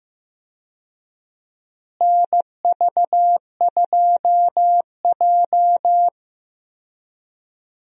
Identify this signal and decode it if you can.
Morse code, heard as a keyed tone, about 15 wpm, 700 Hz. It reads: NV2J
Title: USA Call Signs - 4 Characters - Repeated 3x - 15wpm